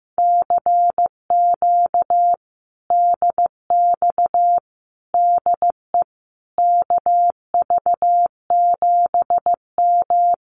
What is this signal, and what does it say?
Morse code, heard as a keyed tone, about 15 wpm, 700 Hz. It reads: CQ DX DE KV7M
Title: Common QSO Phrases Encore 15wpm